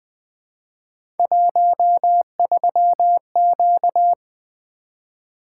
Morse code, heard as a keyed tone, about 20 wpm, 700 Hz. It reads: 13Q
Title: Three Letter-Number Combinations - 20wpm